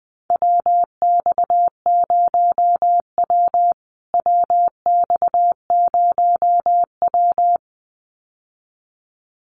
Morse code, heard as a keyed tone, about 20 wpm, 700 Hz. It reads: WX0W WX0W
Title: USA Call Signs Repeated 2x 20wpm